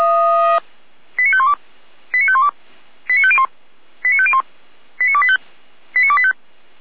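Five-tone selective call (ZVEI), heard as quick runs of five tones, tones of about 70 ms. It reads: 89512 89512 89501 89501 89285 89285
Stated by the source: Five-tone ZVEI selective call 89262, received by SDR.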